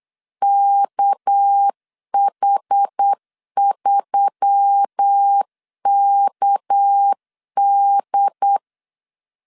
Morse code, heard as a keyed tone, about 8 wpm, 800 Hz. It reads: KH3KD
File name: LEZIONI / 6 PKX0 / 6 PK 8wpm 3003